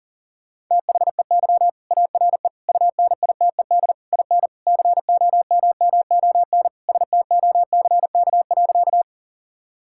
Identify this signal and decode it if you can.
Morse code, heard as a keyed tone, about 37 wpm, 700 Hz. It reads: THEY ARE UNITED IN COMMON STOCK.
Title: Sentences from Top 1000 Words - Review Part 5 - 40wpm